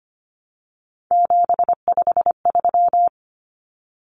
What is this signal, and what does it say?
Morse code, heard as a keyed tone, about 25 wpm, 700 Hz. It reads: 753
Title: Common Cut Numbers - 3-Digit 25wpm